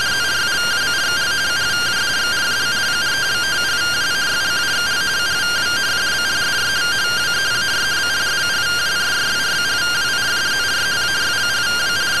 Two-tone teleprinter signal, about 170 Hz shift.